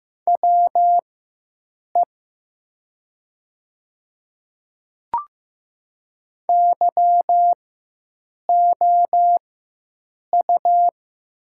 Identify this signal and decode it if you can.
Morse code, heard as a keyed tone, about 15 wpm, spaced as for 7 wpm, 700 Hz. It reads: WE YOU
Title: 4x Character Spacing - Top 100 Words - 15wpm